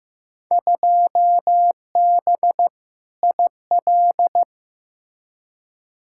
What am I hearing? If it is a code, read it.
Morse code, heard as a keyed tone, about 15 wpm, 700 Hz. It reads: 2B IL
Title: Field Day Exchanges Encore - 15wpm